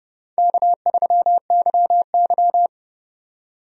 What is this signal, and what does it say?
Morse code, heard as a keyed tone, about 30 wpm, 700 Hz. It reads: K3YY
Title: USA Call Signs - 4 Characters - 30wpm